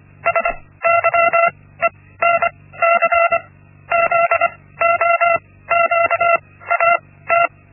Morse code, heard as a keyed tone, about 25 wpm, spaced as for 16 wpm, 680 Hz. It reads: SYENC ZOQAT
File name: Linksmakalnis-1